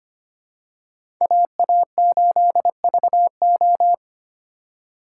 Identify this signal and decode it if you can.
Morse code, heard as a keyed tone, about 25 wpm, 700 Hz. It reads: AA8VO
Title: USA Call Signs - 5 Characters - 2x Character Spacing - 25wpm